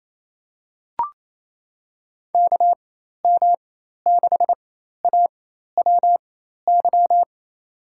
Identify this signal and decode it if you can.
Morse code, heard as a keyed tone, about 28 wpm, spaced as for 13 wpm, 700 Hz. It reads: KM6AWY
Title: USA Call Signs - 6 Characters - 4x Character Spacing - 28wpm